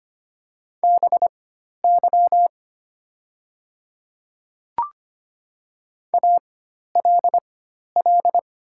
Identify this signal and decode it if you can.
Morse code, heard as a keyed tone, about 25 wpm, spaced as for 12 wpm, 700 Hz. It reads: BY ALL
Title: 4x Character Spacing - Top 100 Words - 25wpm